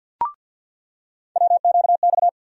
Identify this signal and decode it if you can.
Morse code, heard as a keyed tone, about 47 wpm, 700 Hz. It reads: WQX
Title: USA Call Sign Postfixes - 3 Letters 50wpm